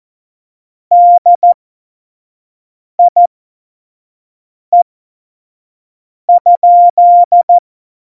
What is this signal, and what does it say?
Morse code, heard as a keyed tone, about 14 wpm, spaced as for 5 wpm, 700 Hz. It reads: DIE?
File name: Morsekurs / Witze / Witz 26. Januar 2021, 25 F70